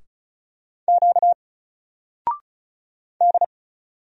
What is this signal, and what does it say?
Morse code, heard as a keyed tone, about 36 wpm, 700 Hz. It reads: Q D
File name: Single-Letter-Number-35wpm